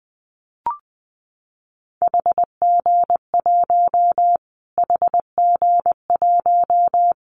Morse code, heard as a keyed tone, about 20 wpm, 700 Hz. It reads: HG1 HG1
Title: Callsign Prefixes for Top 25 DXCC - Repeated 3x - 20wpm